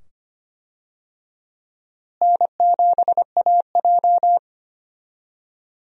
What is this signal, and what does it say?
Morse code, heard as a keyed tone, about 25 wpm, 700 Hz. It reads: N7AJ
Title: USA Call Signs - 4 Characters - Repeated 2x - 25wpm